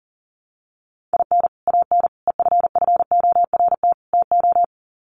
Morse code, heard as a keyed tone, about 40 wpm, 700 Hz. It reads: IN AN EFFORT TO
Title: Sets of 4 Words 40wpm